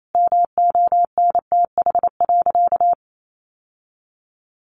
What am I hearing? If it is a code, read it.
Morse code, heard as a keyed tone, about 28 wpm, 700 Hz. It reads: MONTH.
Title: Sentences from Top 800 Words 28wpm